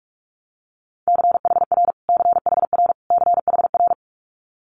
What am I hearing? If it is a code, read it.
Morse code, heard as a keyed tone, about 46 wpm, 700 Hz. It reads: CHR CHR CHR
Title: Multi-letter Phonemes Repeated 3x 45wpm